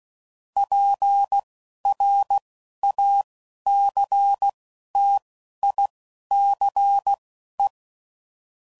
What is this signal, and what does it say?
Morse code, heard as a keyed tone, about 16 wpm, spaced as for 11 wpm, 800 Hz. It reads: PRACTICE
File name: Sayings / Quotes / Deceive